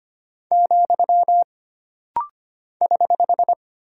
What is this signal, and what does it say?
Morse code, heard as a keyed tone, about 25 wpm, 700 Hz. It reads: , <HH>
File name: Single Punctuation-Prosign - Rapid-Fire - 25wpm